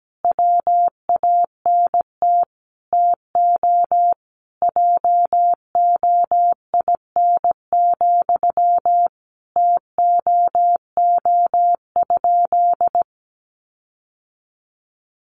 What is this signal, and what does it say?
Morse code, heard as a keyed tone, about 17 wpm, 700 Hz. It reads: WANT TO JOIN, TOO?
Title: Sentences from Top 1400 Words 17wpm